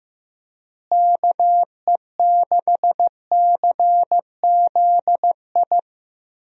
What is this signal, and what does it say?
Morse code, heard as a keyed tone, about 15 wpm, 700 Hz. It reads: KE6CZI
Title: USA Call Signs - 6 Characters - Repeated 2x - 15wpm